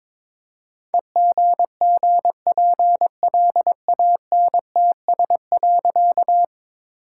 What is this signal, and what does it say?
Morse code, heard as a keyed tone, about 22 wpm, 700 Hz. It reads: EGGPLANTS.
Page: morse-code-ninja-practice-22wpm